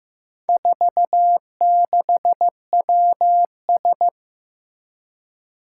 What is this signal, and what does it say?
Morse code, heard as a keyed tone, about 15 wpm, 700 Hz. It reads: 46WS
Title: Four Challenging Letter-Number Combinations 15wpm